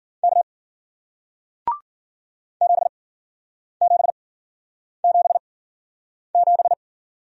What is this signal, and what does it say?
Morse code, heard as a keyed tone, about 53 wpm, spaced as for 9 wpm, 700 Hz. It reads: K 7777